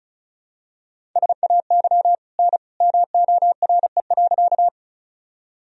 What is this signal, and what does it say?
Morse code, heard as a keyed tone, about 36 wpm, 700 Hz. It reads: SAY N MORE.
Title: Sentences from Top 200 Words and Abbreviations 35wpm